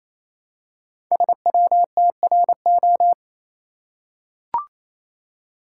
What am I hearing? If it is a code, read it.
Morse code, heard as a keyed tone, about 28 wpm, 700 Hz. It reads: SWTRO
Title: Code Groups - 5 Letters - 28wpm